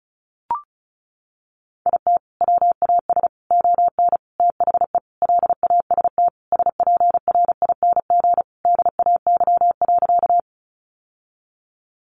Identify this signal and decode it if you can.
Morse code, heard as a keyed tone, about 34 wpm, 700 Hz. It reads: IT WAS ON THE LAST SPRING DAY.
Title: Sentences from Top 1200 Words 35wpm